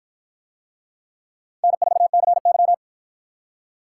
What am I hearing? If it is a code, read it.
Morse code, heard as a keyed tone, about 53 wpm, 700 Hz. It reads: N4CY